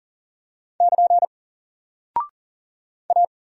Morse code, heard as a keyed tone, about 40 wpm, 700 Hz. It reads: <KN> A
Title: Single Character (Letter:Number:Punctuation:Prosign) Alt 40wpm